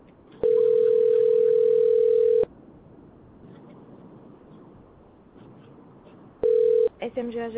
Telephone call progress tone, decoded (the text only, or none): ringback tone